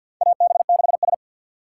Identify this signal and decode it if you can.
Morse code, heard as a keyed tone, about 47 wpm, 700 Hz. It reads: AB6S
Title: USA Call Signs - 2x Character Spacing - 50wpm